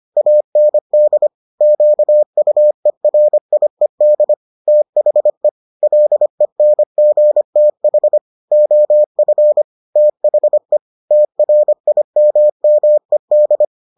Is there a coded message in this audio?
Yes: Morse code from 0.2 to 13.7 s, heard as a keyed tone, about 25 wpm, 600 Hz. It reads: AND QUERIED THE LENGTH OF THE TRIMMED